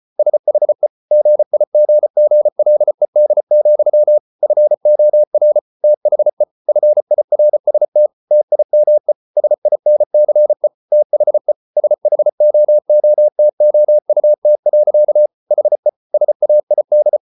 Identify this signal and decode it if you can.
Morse code, heard as a keyed tone, about 34 wpm, 600 Hz. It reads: SHE GIGGLED, FOR THE FIRST TIME SINCE THE SHOOTOUT. HE SAID